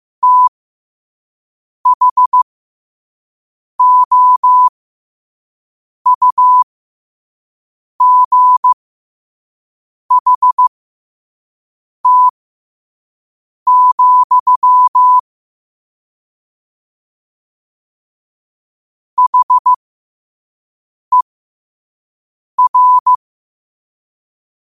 Morse code, heard as a keyed tone, about 15 wpm, spaced as for 5 wpm, 1000 Hz. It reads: THOUGHT, HER
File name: qotdfiles / 5wpm